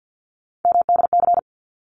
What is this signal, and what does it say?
Morse code, heard as a keyed tone, about 50 wpm, 700 Hz. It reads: MDC
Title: ARRL Sections 50wpm